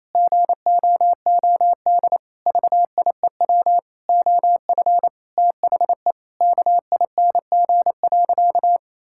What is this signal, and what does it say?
Morse code, heard as a keyed tone, about 28 wpm, 700 Hz. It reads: GOOD VIEW OF THE KING.
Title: Sentences from Top 1000 Words - Review Part 6 - 28wpm